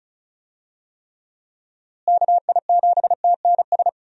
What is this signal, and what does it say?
Morse code, heard as a keyed tone, about 36 wpm, 700 Hz. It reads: KI7TNS
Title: USA Call Signs - 6 Characters - Repeated 3x - 35wpm